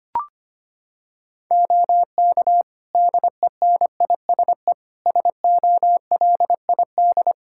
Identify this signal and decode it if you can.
Morse code, heard as a keyed tone, about 25 wpm, 700 Hz. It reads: OK DENISE SOLID
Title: Practice QSO - Exchange 3 - More Info - 25wpm